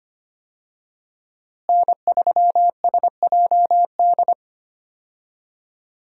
Morse code, heard as a keyed tone, about 25 wpm, 700 Hz. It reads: N3SJD
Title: USA Call Signs - 5 Characters - Repeated 3x - 25wpm